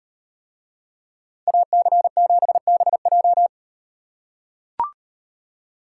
Morse code, heard as a keyed tone, about 38 wpm, 700 Hz. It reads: AC7BJ